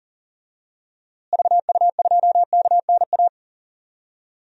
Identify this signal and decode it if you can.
Morse code, heard as a keyed tone, about 38 wpm, 700 Hz. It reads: VU2KNA
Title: DXCC-100 Call Signs - 6 Characters - Repeated 2x - 40wpm